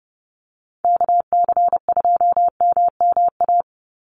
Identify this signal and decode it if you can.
Morse code, heard as a keyed tone, about 30 wpm, 700 Hz. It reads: KC2MMA
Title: USA Call Signs 30wpm